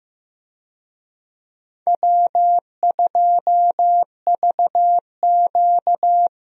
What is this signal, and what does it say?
Morse code, heard as a keyed tone, about 15 wpm, 700 Hz. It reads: W2VQ
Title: USA Call Signs - 4 Characters - Repeated 3x - 15wpm